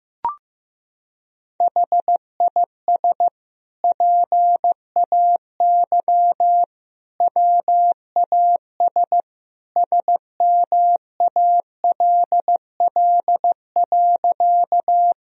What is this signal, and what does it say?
Morse code, heard as a keyed tone, about 15 wpm, 700 Hz. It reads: HIS PAY WAS SMALL.
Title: Sentences from Top 1000 Words - Review Part 1 - 15wpm